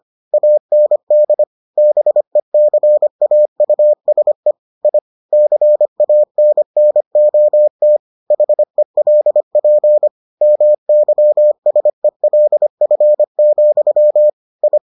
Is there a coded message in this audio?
Yes: Morse code from 0.3 to 14.8 s, heard as a keyed tone, about 25 wpm, 600 Hz. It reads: AND BECAUSE I CANNOT HELP MYSELF, I